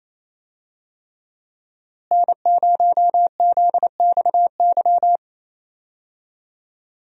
Morse code, heard as a keyed tone, about 28 wpm, 700 Hz. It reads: N0ZXY
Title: USA Call Signs - 5 Characters - Repeated 3x - 28wpm